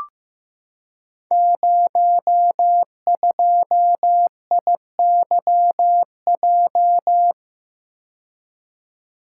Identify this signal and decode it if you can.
Morse code, heard as a keyed tone, about 15 wpm, 700 Hz. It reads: 02IYJ
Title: Code Groups - 5 Letter:Numbers - 15wpm